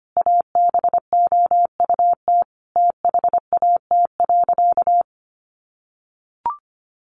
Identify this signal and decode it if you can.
Morse code, heard as a keyed tone, about 25 wpm, 700 Hz. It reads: ABOUT THAT.
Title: Sentences from Top 400 Words 25wpm